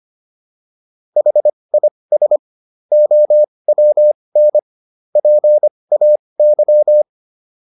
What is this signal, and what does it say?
Morse code, heard as a keyed tone, about 25 wpm, spaced as for 20 wpm, 600 Hz. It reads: HIS OWN PAY